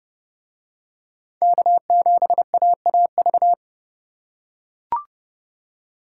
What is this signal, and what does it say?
Morse code, heard as a keyed tone, about 29 wpm, 700 Hz. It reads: K7AAV